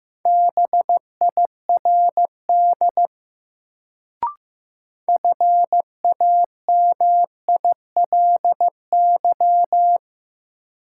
Morse code, heard as a keyed tone, about 15 wpm, 700 Hz. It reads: BIRD FAMILY
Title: Sentences from Top 400 Words - New Word Drill - 15wpm